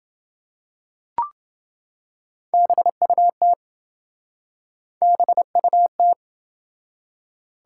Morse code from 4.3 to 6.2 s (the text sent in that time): BUT